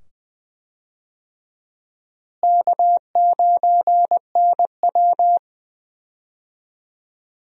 Morse code, heard as a keyed tone, about 20 wpm, 700 Hz. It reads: K9NW
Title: USA Call Signs - 4 Characters - Repeated 2x - 20wpm